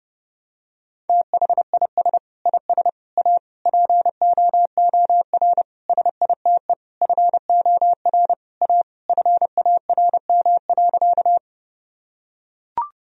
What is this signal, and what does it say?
Morse code, heard as a keyed tone, about 30 wpm, 700 Hz. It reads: THIS IS A POOR SITE FOR A FARM.
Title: Sentences from Top 1000 Words - Review Part 4 - 30wpm